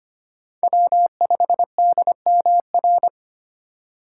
Morse code, heard as a keyed tone, about 25 wpm, 700 Hz. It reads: W5DMR
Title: USA Call Signs - 5 Characters - 3x Character Spacing - 25wpm